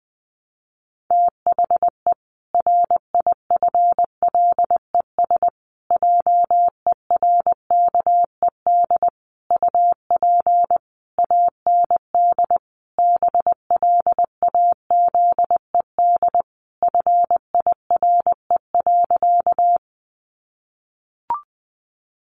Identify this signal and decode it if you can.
Morse code, heard as a keyed tone, about 20 wpm, 700 Hz. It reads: THE RIFLES JERKED UP AND BLAZED FIRE.
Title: A Sound Of Thunder 20wpm